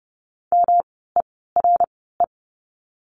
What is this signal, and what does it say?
Morse code, heard as a keyed tone, about 30 wpm, spaced as for 17 wpm, 700 Hz. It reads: MERE
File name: course / 30:10wpm / 108 Lesson 18f - Letter m - Words 30wpm